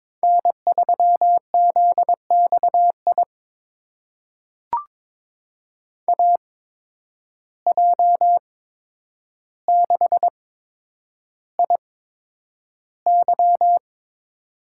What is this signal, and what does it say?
Morse code, heard as a keyed tone, about 22 wpm, spaced as for 9 wpm, 700 Hz. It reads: N3ZXI AJ6IY
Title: USA Call Signs - 5 Characters - 8x Character Spacing - 22wpm